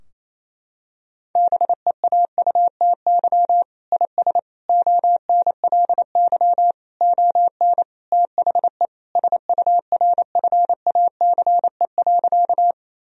Morse code, heard as a keyed tone, about 28 wpm, 700 Hz. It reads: BEAUTY IS ONLY ON THE SURFACE.
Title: Sentences from Top 1000 Words - Review Part 3 - 28wpm